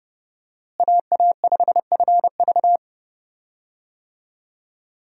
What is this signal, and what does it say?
Morse code, heard as a keyed tone, about 30 wpm, 700 Hz. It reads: AA5FV
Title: USA Call Signs - 5 Characters - Repeated 2x - 30wpm